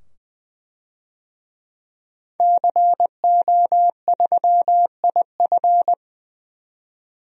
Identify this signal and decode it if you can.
Morse code, heard as a keyed tone, about 20 wpm, 700 Hz. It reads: CO3IF
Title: DXCC-100 Call Signs - 5 Characters - 20wpm